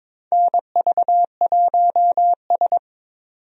Morse code, heard as a keyed tone, about 22 wpm, 700 Hz. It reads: NV1S
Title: USA Call Signs - 4 Characters - 22wpm